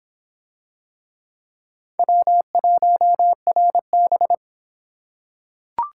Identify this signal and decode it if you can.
Morse code, heard as a keyed tone, about 26 wpm, 700 Hz. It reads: W1RB